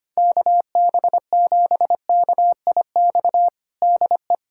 Morse code, heard as a keyed tone, about 25 wpm, 700 Hz. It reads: KB7KIX DE
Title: USA Callsign DE Callsign 25wpm